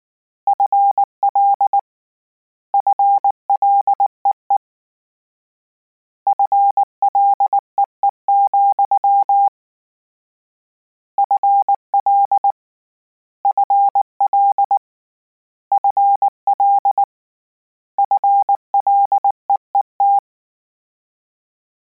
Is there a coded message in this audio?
Yes: Morse code from 0.5 to 20.2 s, heard as a keyed tone, about 19 wpm, 800 Hz. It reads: FL FLEE FLEE, FL FL FL FLEET